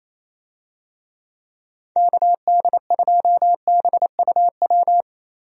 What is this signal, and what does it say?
Morse code, heard as a keyed tone, about 28 wpm, 700 Hz. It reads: KD2BUW
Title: USA Call Signs - 6 Characters - 28wpm